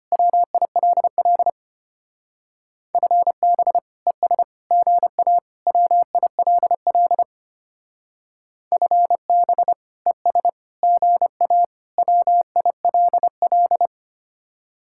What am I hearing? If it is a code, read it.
Morse code, heard as a keyed tone, about 29 wpm, 700 Hz. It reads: WILL FB ES GA WILL FB ES GA WILL
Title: Speed-Racing Common QSO Phrases Encore - 25wpm